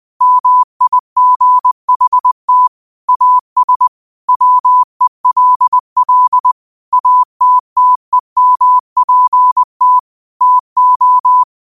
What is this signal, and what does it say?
Morse code, heard as a keyed tone, about 20 wpm, 1000 Hz. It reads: MIGHT AS WELL ATTEMPT TO
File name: qotdfiles / 20wpm